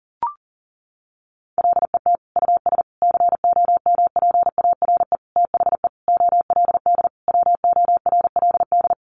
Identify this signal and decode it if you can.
Morse code, heard as a keyed tone, about 40 wpm, 700 Hz. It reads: LET US COMPARE THE OLD WORLD
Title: Sentences from Top 1300 Words 40wpm